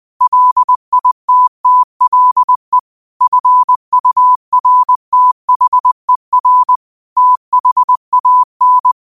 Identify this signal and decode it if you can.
Morse code, heard as a keyed tone, about 20 wpm, 1000 Hz. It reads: LITTLE FURTHER THAN
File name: qotdfiles / 20wpm